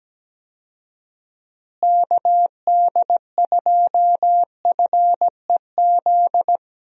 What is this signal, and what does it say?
Morse code, heard as a keyed tone, about 17 wpm, 700 Hz. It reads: KD2FEZ
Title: USA Call Signs - 6 Characters - Repeated 2x - 17wpm